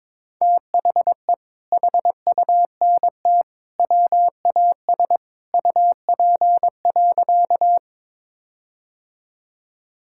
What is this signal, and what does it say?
Morse code, heard as a keyed tone, about 22 wpm, 700 Hz. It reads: THE HUNT WAS UP.
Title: Sentences from Top 1600 Words 22wpm